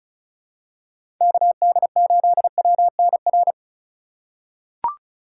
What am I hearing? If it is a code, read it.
Morse code, heard as a keyed tone, about 36 wpm, 700 Hz. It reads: KD8WNR
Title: USA Call Signs - 6 Characters - 2x Character Spacing - 35wpm